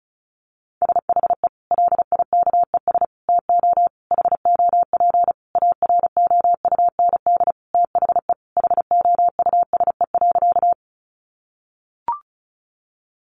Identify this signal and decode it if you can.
Morse code, heard as a keyed tone, about 34 wpm, 700 Hz. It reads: SHE LIKES TO HOP AROUND THE HOUSE.
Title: Easy Phrases 35wpm